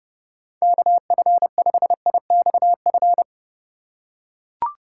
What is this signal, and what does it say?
Morse code, heard as a keyed tone, about 32 wpm, 700 Hz. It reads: KF5IXF
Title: USA Call Signs - 6 Characters - 3x Character Spacing - 30wpm